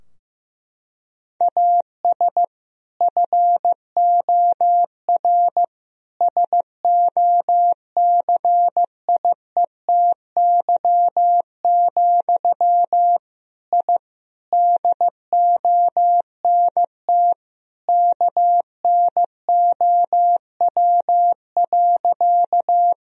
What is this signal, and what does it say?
Morse code, heard as a keyed tone, about 15 wpm, 700 Hz. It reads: AS FOR SOCIETY, I DONT KNOW.
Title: Sentences from Top 1000 Words - Review Part 4 - 15wpm